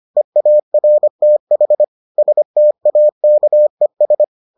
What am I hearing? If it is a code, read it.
Morse code, heard as a keyed tone, about 25 wpm, 600 Hz. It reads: EARTH STAKES